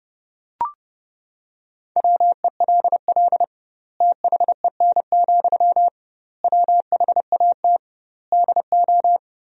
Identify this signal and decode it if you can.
Morse code, heard as a keyed tone, about 30 wpm, 700 Hz. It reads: WELL THEN, WHAT DO
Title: 2x Word-Spacing - Sentences from Top 200 Words - 30wpm